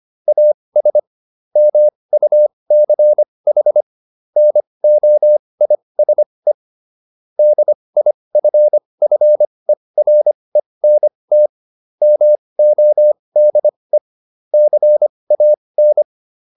Morse code, heard as a keyed tone, about 25 wpm, spaced as for 20 wpm, 600 Hz. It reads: AS MUCH NOISE DIFFERENT MODE CAN